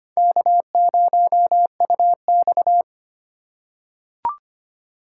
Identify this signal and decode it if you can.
Morse code, heard as a keyed tone, about 25 wpm, 700 Hz. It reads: K0UX
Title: USA Call Signs - 3x Character Spacing - 25wpm